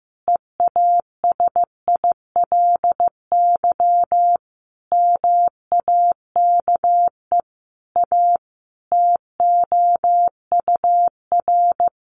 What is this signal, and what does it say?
Morse code, heard as a keyed tone, about 15 wpm, 700 Hz. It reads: EASILY MAKE A TOUR
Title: Sentences from Top 1400 Words 15wpm